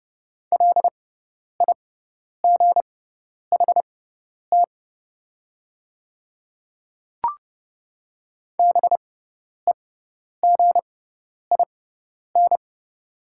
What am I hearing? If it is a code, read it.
Morse code, heard as a keyed tone, about 30 wpm, spaced as for 10 wpm, 700 Hz. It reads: LIGHT BEGIN
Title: 6x Character Spacing - Top 300 Words - 30wpm